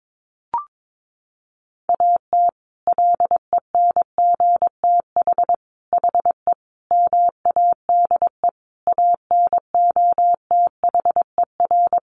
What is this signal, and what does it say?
Morse code, heard as a keyed tone, about 22 wpm, 700 Hz. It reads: AT LENGTH HE MADE ANOTHER
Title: Sentences from Top 1000 Words 22wpm